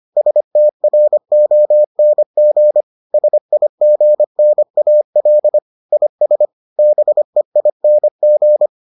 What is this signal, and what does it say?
Morse code, heard as a keyed tone, about 25 wpm, 600 Hz. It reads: STRONG SIGNAL IS BEING